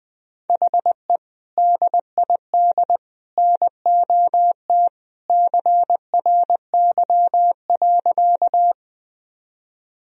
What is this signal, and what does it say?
Morse code, heard as a keyed tone, about 20 wpm, 700 Hz. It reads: HE DID NOT CRY.
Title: Sentences from Top 1000 Words - Review Part 2 - 20wpm